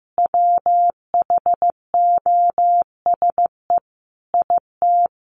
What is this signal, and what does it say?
Morse code, heard as a keyed tone, about 15 wpm, 700 Hz. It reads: WHOSE IT
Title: Sentences from Top 1200 Words 15wpm